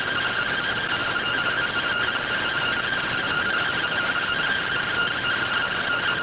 Two-tone teleprinter signal, about 170 Hz shift.